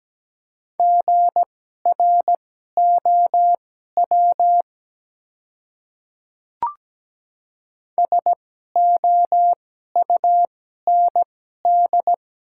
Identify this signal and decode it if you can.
Morse code, heard as a keyed tone, about 17 wpm, spaced as for 12 wpm, 690 Hz. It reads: GROW SOUND
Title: 2x Character Spacing - Top 300 Words - 17wpm